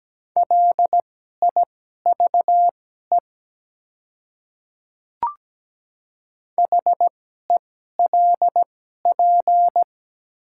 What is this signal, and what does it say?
Morse code, heard as a keyed tone, about 17 wpm, spaced as for 12 wpm, 700 Hz. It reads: LIVE HELP
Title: 2x Character Spacing - Top 200 Words - 17wpm